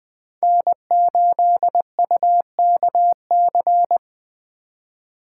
Morse code, heard as a keyed tone, about 20 wpm, 700 Hz. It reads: N8UKC
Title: USA Call Signs - 8x Character Spacing - 20wpm